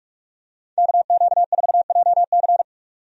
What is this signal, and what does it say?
Morse code, heard as a keyed tone, about 46 wpm, 700 Hz. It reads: KQ4JC
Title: USA Call Signs - 5 Characters - 45wpm